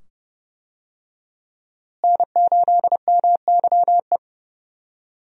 Morse code, heard as a keyed tone, about 30 wpm, 700 Hz. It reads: N8MYE
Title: USA Call Signs - 5 Characters - Repeated 2x - 30wpm